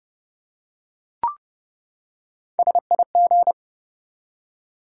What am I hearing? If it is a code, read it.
Morse code, heard as a keyed tone, about 30 wpm, 700 Hz. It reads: SIG
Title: Top 90 QSO Elements 30wpm